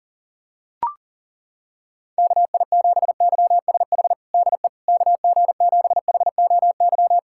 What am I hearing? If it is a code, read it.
Morse code, heard as a keyed tone, about 40 wpm, 700 Hz. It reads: KI7YSH DE KG7HOY
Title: USA Callsign DE Callsign 40wpm